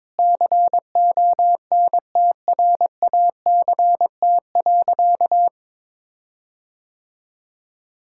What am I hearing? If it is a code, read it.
Morse code, heard as a keyed tone, about 22 wpm, 700 Hz. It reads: CONTRACT.
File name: practice / 22wpm / Sentences with 8-Word Phrases - 22wpm